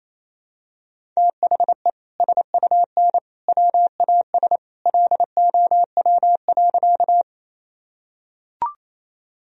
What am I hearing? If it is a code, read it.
Morse code, heard as a keyed tone, about 28 wpm, 700 Hz. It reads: THE SUN WAS LOW.
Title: Sentences from Top 1000 Words - Review Part 1 - 28wpm